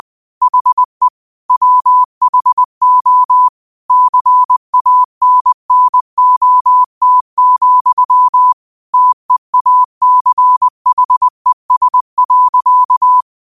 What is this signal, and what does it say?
Morse code, heard as a keyed tone, about 20 wpm, 1000 Hz. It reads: HE WHO CANNOT, TEACHES.
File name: 20wpm